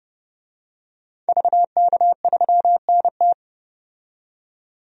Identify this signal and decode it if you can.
Morse code, heard as a keyed tone, about 32 wpm, 700 Hz. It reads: VK3NT
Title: DXCC-100 Call Signs - 5 Characters - 30wpm